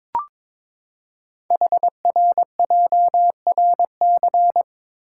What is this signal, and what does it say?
Morse code, heard as a keyed tone, about 23 wpm, 700 Hz. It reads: HRJRC
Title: Code Groups - 5 Letters - 22wpm